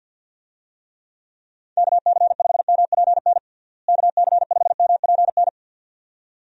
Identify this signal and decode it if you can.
Morse code, heard as a keyed tone, about 51 wpm, 700 Hz. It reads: KC5MPN KC5MPN
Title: USA Call Signs - 6 Characters - Repeated 2x - 50wpm